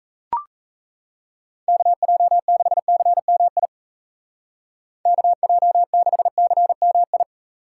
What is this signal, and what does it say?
Morse code, heard as a keyed tone, about 34 wpm, 700 Hz. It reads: KJ6CMI KJ6CMI
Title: Speed Racing - USA Call Signs 28wpm